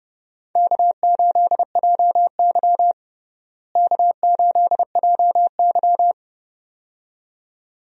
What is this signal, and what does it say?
Morse code, heard as a keyed tone, about 30 wpm, 700 Hz. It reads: K8JY K8JY
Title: USA Call Signs - 4 Characters - Repeated 2x - 30wpm